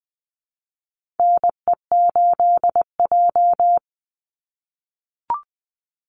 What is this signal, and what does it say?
Morse code, heard as a keyed tone, about 20 wpm, 700 Hz. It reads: NE8J
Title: Call Signs - 20wpm